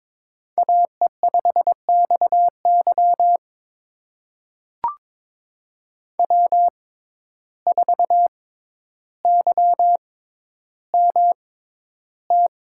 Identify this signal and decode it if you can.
Morse code, heard as a keyed tone, about 21 wpm, spaced as for 11 wpm, 700 Hz. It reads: AE5XY W4YMT
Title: USA Call Signs - 5 Characters - 6x Character Spacing - 22wpm